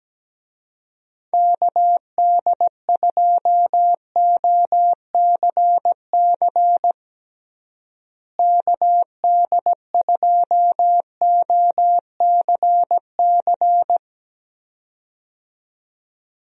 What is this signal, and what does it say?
Morse code, heard as a keyed tone, about 17 wpm, 700 Hz. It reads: KD2OCC KD2OCC
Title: USA Call Signs - 6 Characters - Repeated 2x - 17wpm